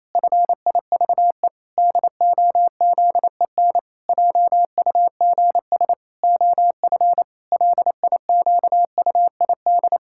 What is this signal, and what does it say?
Morse code, heard as a keyed tone, about 27 wpm, 700 Hz. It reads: FIVE DOZEN JUGS OF LIQUID
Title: Pangram Challenge v2 - 28wpm